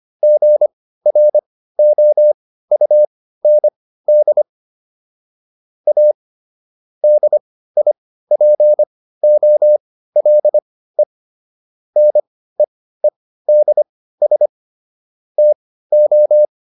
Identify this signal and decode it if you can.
Morse code, heard as a keyed tone, about 25 wpm, spaced as for 15 wpm, 600 Hz. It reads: GROUND A DIPOLE NEEDS TO